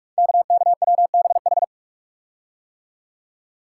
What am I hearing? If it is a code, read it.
Morse code, heard as a keyed tone, about 44 wpm, 690 Hz. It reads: KKWBH
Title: Code Groups - 5 Letters - 45wpm